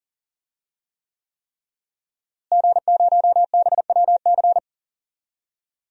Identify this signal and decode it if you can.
Morse code, heard as a keyed tone, about 40 wpm, 700 Hz. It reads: G0BWC
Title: DXCC-100 Call Signs - 5 Characters - Repeated 3x - 40wpm